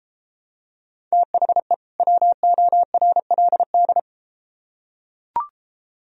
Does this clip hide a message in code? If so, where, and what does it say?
Yes: Morse code from 1.1 to 4.0 s, heard as a keyed tone, about 33 wpm, 700 Hz. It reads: THE WORLD